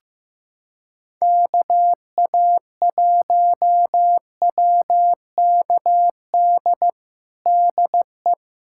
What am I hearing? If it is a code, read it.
Morse code, heard as a keyed tone, about 16 wpm, 700 Hz. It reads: KA1WKD DE
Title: USA Callsign DE Callsign - 6 Characters - 15wpm